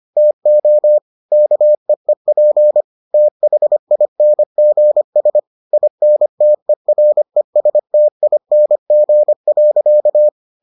Morse code, heard as a keyed tone, about 25 wpm, 600 Hz. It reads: TO KEEP THINGS INTERESTING.